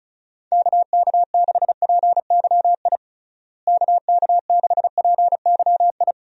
Morse code, heard as a keyed tone, about 34 wpm, 700 Hz. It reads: KK6PYI KK6PYI
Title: USA Call Signs - 6 Characters - Repeated 3x - 35wpm